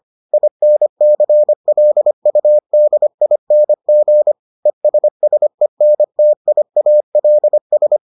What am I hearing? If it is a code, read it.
Morse code, heard as a keyed tone, about 25 wpm, 600 Hz. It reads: INCLUDING ESSENTIALS